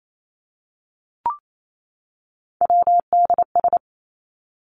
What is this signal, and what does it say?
Morse code, heard as a keyed tone, about 28 wpm, 700 Hz. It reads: WDS
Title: Abbreviations 28wpm